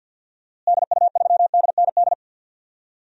Morse code, heard as a keyed tone, about 53 wpm, 700 Hz. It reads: DU3DND
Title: DXCC-100 Call Signs - 6 Characters - Repeated 2x - 50wpm